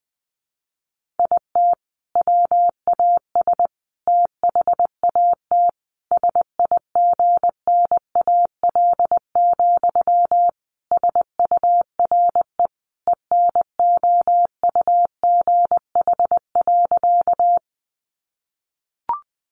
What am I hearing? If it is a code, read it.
Morse code, heard as a keyed tone, about 19 wpm, 700 Hz. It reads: IT WAS THAT SIGNAL, SURE ENOUGH.
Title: Sentences from Top 1400 Words 20wpm